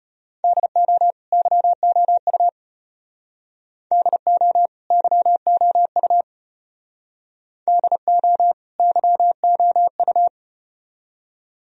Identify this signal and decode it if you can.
Morse code, heard as a keyed tone, about 34 wpm, 700 Hz. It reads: DO YOU DO YOU DO YOU